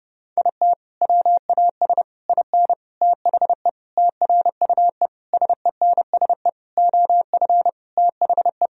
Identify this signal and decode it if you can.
Morse code, heard as a keyed tone, about 32 wpm, 700 Hz. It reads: IT WAS IN THE TRUE SENSE OF THE
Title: Sentences from Top 1000 Words - Review Part 4 - 30wpm